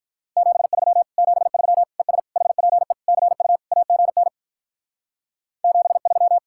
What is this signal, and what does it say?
Morse code, heard as a keyed tone, about 44 wpm, 700 Hz. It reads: 73 73 ES HPE CU AGN 73
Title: Speed-Racing Common QSO Phrases - 35wpm